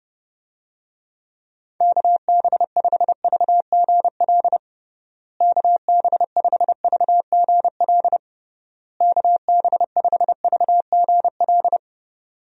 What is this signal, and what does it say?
Morse code, heard as a keyed tone, about 30 wpm, 700 Hz. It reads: KB5VGL KB5VGL KB5VGL
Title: USA Call Signs - 6 Characters - Repeated 3x - 30wpm